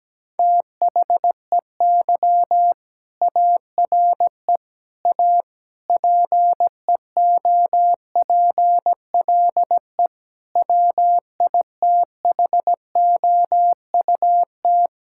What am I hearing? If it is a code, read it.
Morse code, heard as a keyed tone, about 17 wpm, 700 Hz. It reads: THEY ARE A PEOPLE WITHOUT
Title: Sentences from Top 1000 Words - Review Part 6 - 17wpm